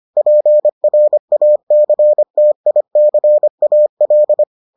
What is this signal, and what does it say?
Morse code, heard as a keyed tone, about 25 wpm, 600 Hz. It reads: PRACTICAL